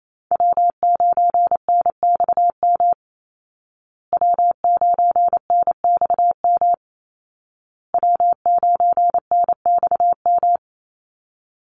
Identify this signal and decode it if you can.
Morse code, heard as a keyed tone, about 28 wpm, 700 Hz. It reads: W9NXM W9NXM W9NXM
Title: Call Signs Repeated 2x - 28wpm